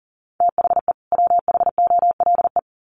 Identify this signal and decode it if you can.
Morse code, heard as a keyed tone, about 40 wpm, 700 Hz. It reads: THE WHOLE
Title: Sets of 2 Words 40wpm